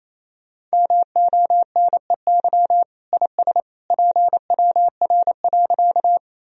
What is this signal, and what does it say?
Morse code, heard as a keyed tone, about 28 wpm, 700 Hz. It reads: MONEY IS PWR.
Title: Sentences from Top 500 Words and Abbreviations 28wpm